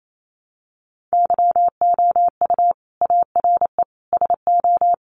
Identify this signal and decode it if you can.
Morse code, heard as a keyed tone, about 28 wpm, 700 Hz. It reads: YOU ARE SO
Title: Sentences from Top 1200 Words 28wpm